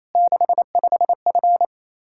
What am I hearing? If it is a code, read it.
Morse code, heard as a keyed tone, about 28 wpm, 700 Hz. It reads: 65F
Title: Practice QSO - Exchange 2 - Info - 28wpm